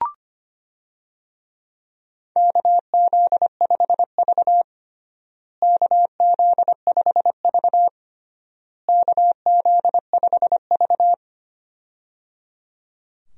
Morse code, heard as a keyed tone, about 25 wpm, 700 Hz. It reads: KZ5V KZ5V KZ5V
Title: USA Call Signs - 4 Characters - Repeated 3x - 25wpm